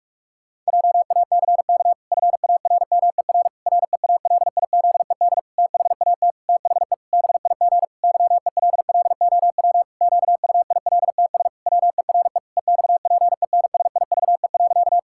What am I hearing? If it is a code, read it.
Morse code, heard as a keyed tone, about 46 wpm, 700 Hz. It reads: JACK FARMER REALIZED THAT THE BIG YELLOW QUILTS WERE EXPENSIVE.
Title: Pangram Challenge v2 - 45wpm